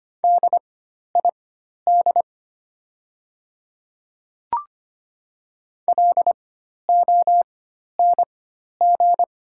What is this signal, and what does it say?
Morse code, heard as a keyed tone, about 25 wpm, spaced as for 12 wpm, 700 Hz. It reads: DID LONG
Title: 4x Character Spacing - Top 100 Words - 25wpm